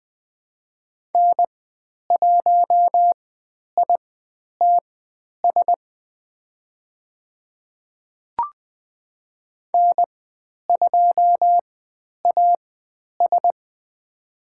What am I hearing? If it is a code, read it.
Morse code, heard as a keyed tone, about 20 wpm, spaced as for 10 wpm, 700 Hz. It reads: N1ITS N2AS
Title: 102 Lesson 17g - Letter u - Call Signs 20wpm